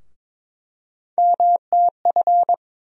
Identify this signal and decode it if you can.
Morse code, heard as a keyed tone, about 21 wpm, 700 Hz. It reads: MTF
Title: USA Call Sign Postfixes - 3 Letters 22wpm